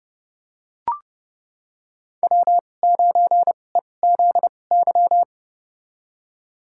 Morse code, heard as a keyed tone, about 30 wpm, spaced as for 22 wpm, 700 Hz. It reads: W9EZY
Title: USA Call Signs - 5 Characters - 2x Character Spacing - 30wpm